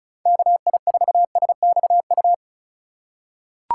Morse code, heard as a keyed tone, about 36 wpm, 700 Hz. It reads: KI4SXU
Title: USA Call Signs - 6 Characters - 2x Character Spacing - 35wpm